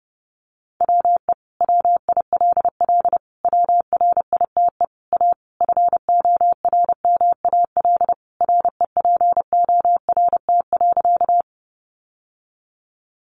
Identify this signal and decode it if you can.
Morse code, heard as a keyed tone, about 30 wpm, 700 Hz. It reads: WE WILL WRITE A FORMAL REPORT.
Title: Sentences from Top 1300 Words 30wpm